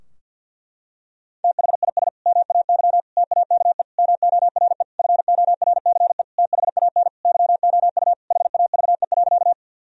Morse code, heard as a keyed tone, about 49 wpm, 700 Hz. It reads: THIS MAY TAKE MORE FORCE THAN YOU HAVE.
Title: Sentences from Top 1000 Words - Review Part 3 - 50wpm